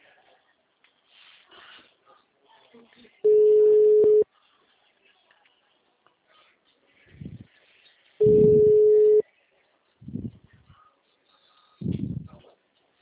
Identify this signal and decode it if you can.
Ringback tone (Europe, 425 Hz), with speech.